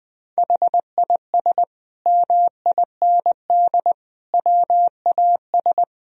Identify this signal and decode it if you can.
Morse code, heard as a keyed tone, about 21 wpm, 700 Hz. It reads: HIS MIND WAS
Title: Sentences from Top 1000 Words - Review Part 1 - 20wpm